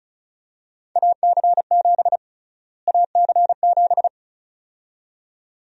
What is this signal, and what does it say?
Morse code, heard as a keyed tone, about 34 wpm, 700 Hz. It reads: AC7 AC7
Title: USA Call Sign Prefixes - Repeated 2x - 35wpm